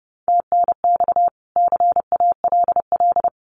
Morse code, heard as a keyed tone, about 30 wpm, 700 Hz. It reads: TNX CALL
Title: Common QSO Phrases 30wpm